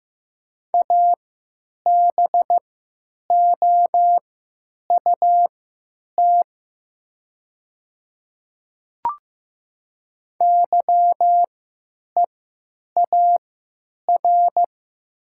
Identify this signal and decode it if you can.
Morse code, heard as a keyed tone, about 15 wpm, spaced as for 9 wpm, 700 Hz. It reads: ABOUT YEAR
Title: 3x Character Spacing - Top 200 Words - 15wpm